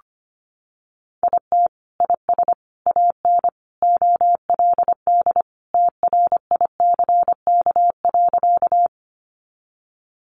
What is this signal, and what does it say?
Morse code, heard as a keyed tone, about 25 wpm, 700 Hz. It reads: IT IS AN OLD TRICK.
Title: Sentences from Top 1500 Words 25wpm